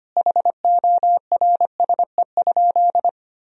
Morse code, heard as a keyed tone, about 25 wpm, 700 Hz. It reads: HORSE?
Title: Sentences from Top 1000 Words - Review Part 1 - 25wpm